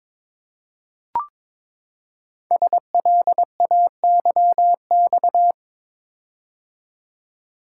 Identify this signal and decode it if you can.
Morse code, heard as a keyed tone, about 22 wpm, 700 Hz. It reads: SLAYX
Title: Code Groups - 5 Letters - 22wpm